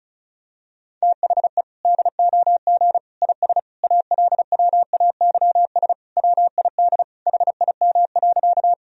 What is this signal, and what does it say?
Morse code, heard as a keyed tone, about 36 wpm, 700 Hz. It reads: THE DOG IS ALWAYS WID HIM.
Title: Sentences from Top 400 Words and Abbreviations 35wpm